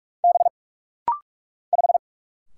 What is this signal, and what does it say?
Morse code, heard as a keyed tone, about 44 wpm, 700 Hz. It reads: B 5
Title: Single Letter-Number Alt - Rapid-Fire 45wpm